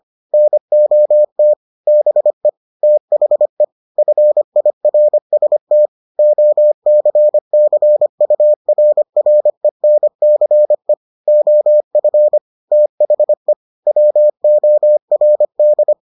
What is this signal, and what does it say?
Morse code, heard as a keyed tone, about 25 wpm, 600 Hz. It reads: NOT BE THE FIRST OCCURRENCE OF THE WORD